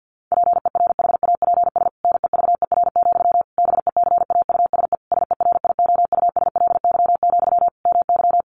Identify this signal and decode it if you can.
Morse code, heard as a keyed tone, about 50 wpm, 700 Hz. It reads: PERHAPS NEVER, BECAUSE SERIOUSLY, MY
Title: Sentences from Top 1200 Words 50wpm